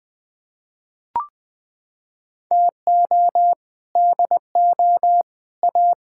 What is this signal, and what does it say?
Morse code, heard as a keyed tone, about 20 wpm, 700 Hz. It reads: TO DO A
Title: Sets of 6 Words - 20wpm